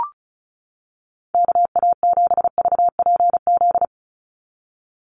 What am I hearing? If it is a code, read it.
Morse code, heard as a keyed tone, about 34 wpm, 700 Hz. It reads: KA7VPZ
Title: USA Call Signs 35wpm